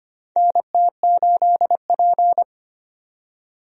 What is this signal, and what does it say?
Morse code, heard as a keyed tone, about 25 wpm, 700 Hz. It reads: NT8P
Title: USA Call Signs - 4 Characters - 25wpm